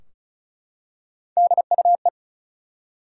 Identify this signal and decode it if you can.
Morse code, heard as a keyed tone, about 34 wpm, 700 Hz. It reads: DUE
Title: Three Letter Words - 35wpm